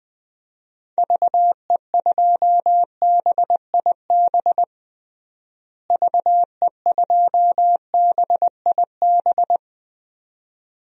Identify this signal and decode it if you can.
Morse code, heard as a keyed tone, about 21 wpm, 700 Hz. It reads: VE2BIB VE2BIB
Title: DXCC-100 Call Signs - 6 Characters - Repeated 2x - 20wpm